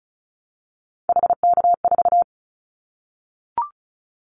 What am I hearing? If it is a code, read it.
Morse code, heard as a keyed tone, about 36 wpm, 700 Hz. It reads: HK4
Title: Callsign Prefixes for Top 100 DXCC - 35wpm